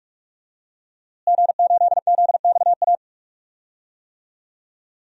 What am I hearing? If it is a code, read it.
Morse code, heard as a keyed tone, about 46 wpm, 700 Hz. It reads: G8ZXA
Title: DXCC-100 Call Signs - 5 Characters - Repeated 3x - 45wpm